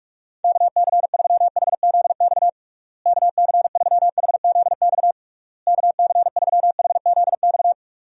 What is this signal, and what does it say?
Morse code, heard as a keyed tone, about 46 wpm, 700 Hz. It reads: KC3HZX KC3HZX KC3HZX
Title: USA Call Signs - 6 Characters - Repeated 3x - 45wpm